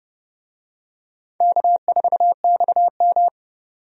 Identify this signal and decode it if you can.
Morse code, heard as a keyed tone, about 30 wpm, 700 Hz. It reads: K4XM
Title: USA Call Signs - 4 Characters - 30wpm